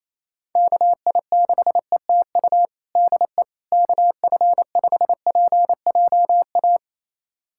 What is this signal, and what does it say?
Morse code, heard as a keyed tone, about 28 wpm, 700 Hz. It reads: KI6ETU DE KF5PJA
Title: USA Callsign DE Callsign - 6 Characters - 28wpm